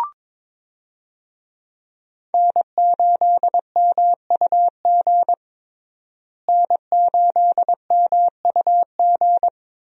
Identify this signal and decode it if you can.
Morse code, heard as a keyed tone, about 22 wpm, 700 Hz. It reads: N8MUG N8MUG
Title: USA Call Signs - 5 Characters - Repeated 3x - 22wpm